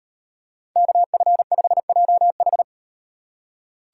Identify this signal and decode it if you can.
Morse code, heard as a keyed tone, about 38 wpm, 700 Hz. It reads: KF5JH